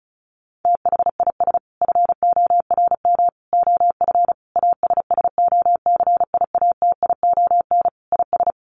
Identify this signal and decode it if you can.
Morse code, heard as a keyed tone, about 34 wpm, 700 Hz. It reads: THIS FORM OF ASSOCIATION IS
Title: Sentences from Top 1100 Words 35wpm